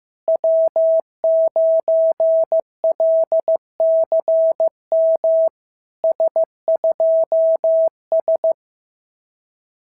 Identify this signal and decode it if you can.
Morse code, heard as a keyed tone, about 14 wpm, 650 Hz. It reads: W9LCM S2S
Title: SOTA QSO Practice 15wpm